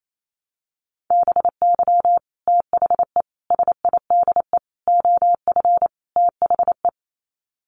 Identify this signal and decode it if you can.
Morse code, heard as a keyed tone, about 28 wpm, 700 Hz. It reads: BY THE SIDE OF THE
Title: Sets of 5 Words Encore2 28wpm